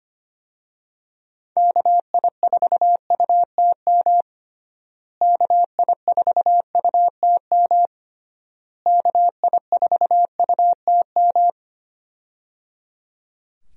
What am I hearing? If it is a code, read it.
Morse code, heard as a keyed tone, about 25 wpm, 700 Hz. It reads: KI4UTM KI4UTM KI4UTM
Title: USA Call Signs - 6 Characters - Repeated 3x - 25wpm